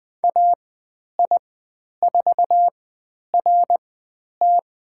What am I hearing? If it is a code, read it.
Morse code, heard as a keyed tone, about 20 wpm, spaced as for 10 wpm, 700 Hz. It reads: AI4RT
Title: 102 Lesson 17g - Letter u - Call Signs 20wpm